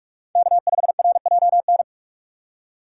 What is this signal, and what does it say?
Morse code, heard as a keyed tone, about 41 wpm, 700 Hz. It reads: K5RJN
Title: USA Call Signs - 5 Characters - 2x Character Spacing - 45wpm